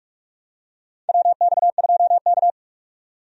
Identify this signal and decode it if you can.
Morse code, heard as a keyed tone, about 46 wpm, 700 Hz. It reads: WX2K
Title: USA Call Signs - 4 Characters - 45wpm